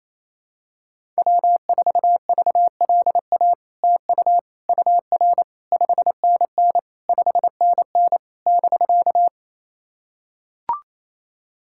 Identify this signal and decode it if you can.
Morse code, heard as a keyed tone, about 28 wpm, 700 Hz. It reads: W4VLA TU UR 5NN 5NN <BK>
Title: SOTA QSO Practice 28wpm